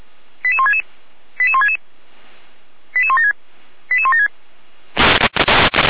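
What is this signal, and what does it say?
Five-tone selective call (ZVEI), heard as quick runs of five tones, tones of about 80 ms. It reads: 88166 88166 88176 88176
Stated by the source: ZVEI five-tone selective call received on an SDR.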